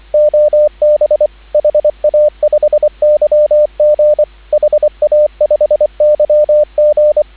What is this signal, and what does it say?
Morse code, heard as a keyed tone, about 24 wpm, 600 Hz, with radio noise. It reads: OB HA5YG HA5YG
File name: CW / 2017 HA-OB HA5YG